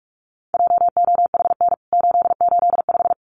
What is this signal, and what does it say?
Morse code, heard as a keyed tone, about 44 wpm, 700 Hz. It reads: JOHN 885
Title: CWT Exchange 45wpm